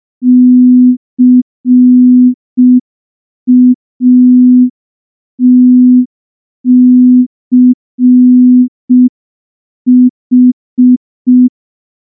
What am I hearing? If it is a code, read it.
Morse code, heard as a keyed tone, about 5 wpm, 250 Hz. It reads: CATCH
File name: morse / morse-training-05WPM